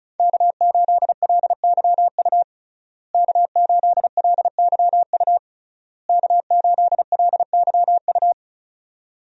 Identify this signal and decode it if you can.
Morse code, heard as a keyed tone, about 36 wpm, 700 Hz. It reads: K8LYU K8LYU K8LYU
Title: USA Call Signs - 5 Characters - Repeated 3x - 35wpm